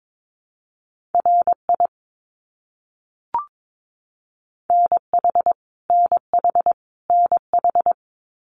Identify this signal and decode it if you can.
Morse code, heard as a keyed tone, about 22 wpm, 700 Hz. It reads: RI NH NH NH
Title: US States Repeated 3x 22wpm